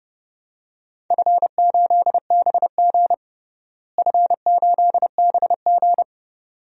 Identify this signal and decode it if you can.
Morse code, heard as a keyed tone, about 32 wpm, 700 Hz. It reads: F8BG F8BG
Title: DXCC-100 Call Signs - 4 Characters - Repeated 2x - 30wpm